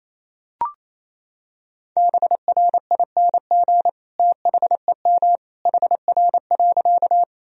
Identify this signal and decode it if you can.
Morse code, heard as a keyed tone, about 28 wpm, 700 Hz. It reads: BRING THEM HR.
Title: Sentences from Top 500 Words and Abbreviations 28wpm